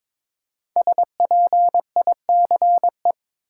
Morse code, heard as a keyed tone, about 22 wpm, 700 Hz. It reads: SPICE
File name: 5 Letter Words - First 3 Letters Given - 22wpm